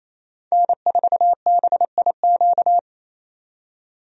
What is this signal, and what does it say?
Morse code, heard as a keyed tone, about 28 wpm, 700 Hz. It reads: N4BIQ
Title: USA Call Signs - 5 Characters - 28wpm